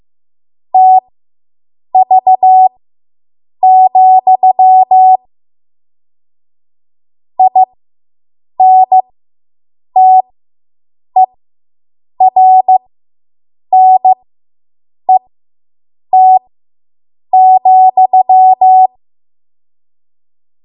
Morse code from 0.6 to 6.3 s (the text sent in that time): TV,